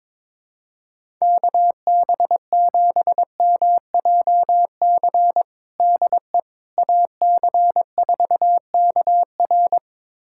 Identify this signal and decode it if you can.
Morse code, heard as a keyed tone, about 22 wpm, 700 Hz. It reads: KB7MJC DE AC4KR
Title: USA Callsign DE Callsign 22wpm